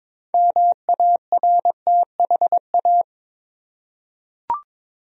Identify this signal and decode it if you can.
Morse code, heard as a keyed tone, about 21 wpm, 700 Hz. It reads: MARTHA
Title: USA Common Names - 22wpm